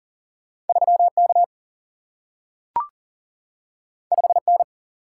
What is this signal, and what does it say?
Morse code, heard as a keyed tone, about 40 wpm, 700 Hz. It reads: 3K 5N
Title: Single-Letter Single-Number 40wpm